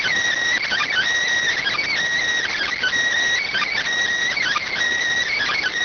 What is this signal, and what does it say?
Emergency Alert System (SAME) tones (header bursts), heard as harsh buzzing bursts. It is heard from 0.5 to 4.7 s.
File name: findings / CT 205 ax mips, iDose 3 0006.gen-42 rx